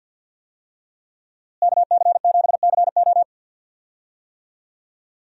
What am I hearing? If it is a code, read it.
Morse code, heard as a keyed tone, about 49 wpm, 700 Hz. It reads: KC7CO
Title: USA Call Signs - 5 Characters - Repeated 2x - 50wpm